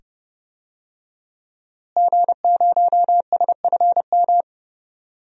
Morse code, heard as a keyed tone, about 30 wpm, 700 Hz. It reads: G0SFM
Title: DXCC-100 Call Signs - 5 Characters - Repeated 2x - 30wpm